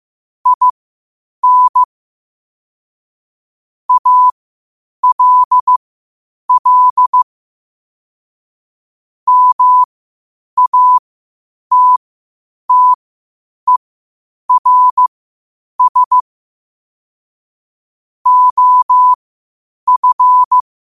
Morse code, heard as a keyed tone, about 15 wpm, spaced as for 8 wpm, 1000 Hz. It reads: IN ALL MATTERS OF
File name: qotdfiles / 8wpm